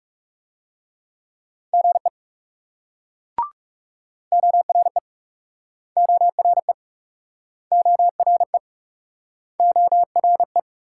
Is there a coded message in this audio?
Yes: Morse code from 1.7 to 10.6 s, heard as a keyed tone, about 38 wpm, 700 Hz. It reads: GE ORE ORE ORE ORE